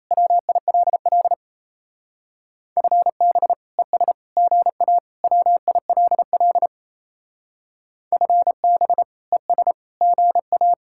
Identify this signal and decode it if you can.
Morse code, heard as a keyed tone, about 36 wpm, 700 Hz. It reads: WILL FB ES GA WILL FB ES GA